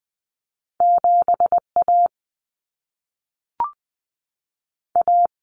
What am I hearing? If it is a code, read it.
Morse code, heard as a keyed tone, about 20 wpm, 700 Hz. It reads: 7A A
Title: Common Cut Numbers - 2-Digit 20wpm